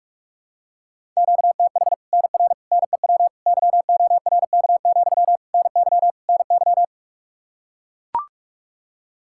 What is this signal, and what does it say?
Morse code, heard as a keyed tone, about 42 wpm, 700 Hz. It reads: QTH NR NEW YORK, NY NY
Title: Practice QSO - Exchange 1 - Beginning - 45wpm